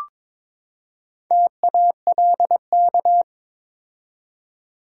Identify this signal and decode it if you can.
Morse code, heard as a keyed tone, about 22 wpm, 700 Hz. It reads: TALK
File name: practice / 22wpm / Build A Phrase - 5-Word Phrases - 22wpm